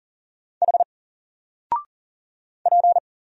Morse code, heard as a keyed tone, about 40 wpm, 700 Hz. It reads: H P
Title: Single Letter-Number 40wpm